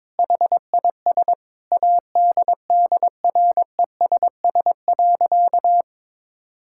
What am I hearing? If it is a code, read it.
Morse code, heard as a keyed tone, about 22 wpm, 700 Hz. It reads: HIS ADDRESS.
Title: Sentences from Top 1000 Words - Review Part 6 - 22wpm